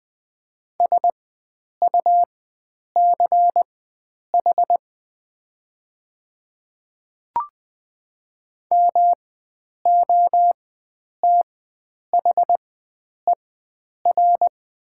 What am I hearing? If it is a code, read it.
Morse code, heard as a keyed tone, about 20 wpm, spaced as for 9 wpm, 700 Hz. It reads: SUCH MOTHER
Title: 4x Character Spacing - Top 500 Words - 20wpm